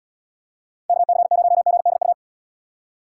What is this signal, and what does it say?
Morse code, heard as a keyed tone, about 78 wpm, 700 Hz. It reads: KC0GDU